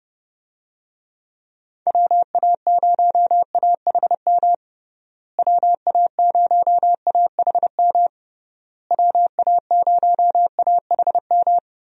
Morse code, heard as a keyed tone, about 32 wpm, 700 Hz. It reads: WA0AHM WA0AHM WA0AHM
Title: USA Call Signs - 6 Characters - Repeated 3x - 30wpm